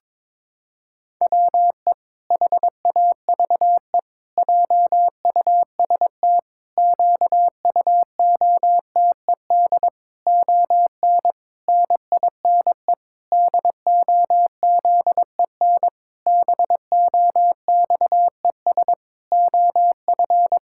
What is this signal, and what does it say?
Morse code, heard as a keyed tone, about 22 wpm, 700 Hz. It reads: WE HAVE JUST QUOTED ON NINE DOZEN BOXES OF
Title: Pangram Challenge v2 - 22wpm